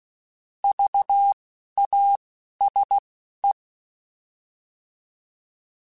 Morse code, heard as a keyed tone, about 16 wpm, spaced as for 11 wpm, 800 Hz. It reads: VASE
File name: Alphabetical / Alphabetical mp3s / Vs @10